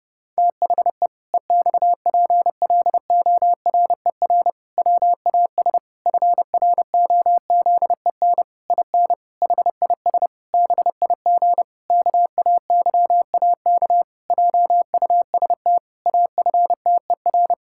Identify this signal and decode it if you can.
Morse code, heard as a keyed tone, about 30 wpm, 700 Hz. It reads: THE EXPLORER WAS FROZEN IN HIS BIG KAYAK JUST AFTER
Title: Pangram Challenge v2 - 30wpm